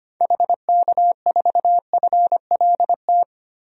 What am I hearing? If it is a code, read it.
Morse code, heard as a keyed tone, about 25 wpm, 700 Hz. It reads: HK4FLT
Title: DXCC-100 Call Signs - 6 Characters - 25wpm